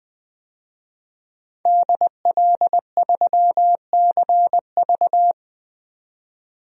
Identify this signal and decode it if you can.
Morse code, heard as a keyed tone, about 20 wpm, 700 Hz. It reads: DL3CV
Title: DXCC-100 Call Signs - 5 Characters - 20wpm